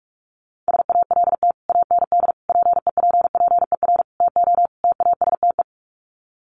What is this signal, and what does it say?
Morse code, heard as a keyed tone, about 44 wpm, 700 Hz. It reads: SALT AND PEPPER TO TASTE
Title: Sets of 5 Words 45wpm